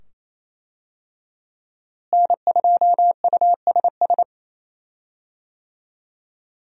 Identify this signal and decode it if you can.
Morse code, heard as a keyed tone, about 28 wpm, 700 Hz. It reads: N2USS
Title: USA Call Signs - 5 Characters - Repeated 2x - 28wpm